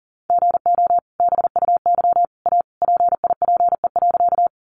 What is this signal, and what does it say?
Morse code, heard as a keyed tone, about 40 wpm, 700 Hz. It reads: GO BUY A PIPE.
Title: Sentences from Top 1500 Words 40wpm